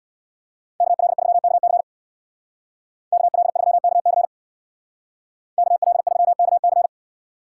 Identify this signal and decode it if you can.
Morse code, heard as a keyed tone, about 64 wpm, 700 Hz. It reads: KL3KC KL3KC KL3KC